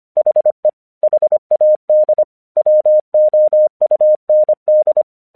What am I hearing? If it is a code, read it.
Morse code, heard as a keyed tone, about 25 wpm, 610 Hz. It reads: HE HAD WOUND
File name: 1 English 25wpm / AQC24